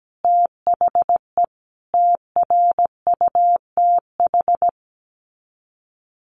Morse code, heard as a keyed tone, about 17 wpm, 700 Hz. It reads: THE TRUTH
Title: Sets of 5 Words 17wpm